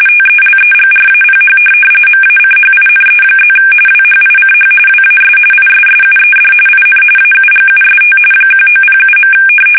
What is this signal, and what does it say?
RTTY, about 45.45 baud. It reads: TELEGRAPH MACHINE+ WHICH WAS ISSUED IN AUGUST 1907.
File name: fsk850 / teletype-corp